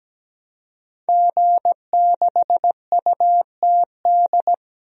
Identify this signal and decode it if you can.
Morse code, heard as a keyed tone, about 17 wpm, 700 Hz. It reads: G6UTD
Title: DXCC-100 Call Signs - 5 Characters - 17wpm